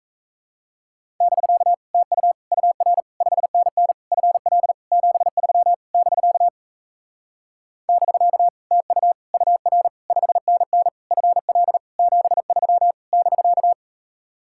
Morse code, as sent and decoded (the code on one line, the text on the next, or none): -...-.- / - ..- / ..- .-. / ..... -. -. / ..-. .-.. / --... ...-- / -...-.- / -...-.- / - ..- / ..- .-. / ..... -. -. / ..-. .-.. / --... ...-- / -...-.-
<BK> TU UR 5NN FL 73 <BK> <BK> TU UR 5NN FL 73 <BK>